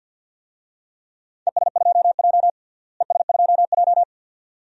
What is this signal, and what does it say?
Morse code, heard as a keyed tone, about 45 wpm, 700 Hz. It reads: ES2J ES2J
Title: DXCC-100 Call Signs - 4 Characters - Repeated 2x - 50wpm